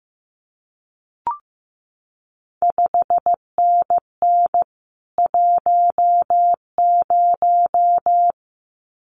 Morse code, heard as a keyed tone, about 15 wpm, 700 Hz. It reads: 5NN 10
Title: CQ WW DX Contest Exchange Encore - 15wpm